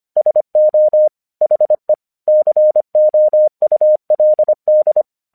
Morse code, heard as a keyed tone, about 25 wpm, 610 Hz. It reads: SO HE COULD
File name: 1 English 25wpm / AQC26